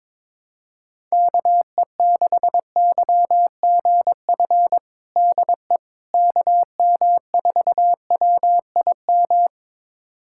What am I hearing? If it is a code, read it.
Morse code, heard as a keyed tone, about 22 wpm, 700 Hz. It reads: KE6YGF DE KM4WIM
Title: USA Callsign DE Callsign - 6 Characters - 22wpm